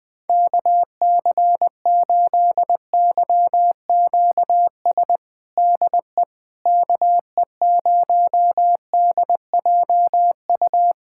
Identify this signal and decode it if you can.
Morse code, heard as a keyed tone, about 20 wpm, 700 Hz. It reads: KC8YQS DE KE0DJU
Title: USA Callsign DE Callsign 20wpm